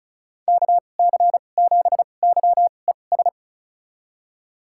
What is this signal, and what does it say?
Morse code, heard as a keyed tone, about 36 wpm, spaced as for 25 wpm, 700 Hz. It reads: KC7YES
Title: USA Call Signs - 2x Character Spacing - 35wpm